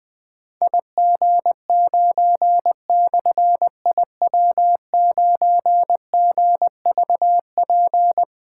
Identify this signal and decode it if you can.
Morse code, heard as a keyed tone, about 20 wpm, 700 Hz. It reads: IG9/IW9GVP
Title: Difficult Call Signs 20wpm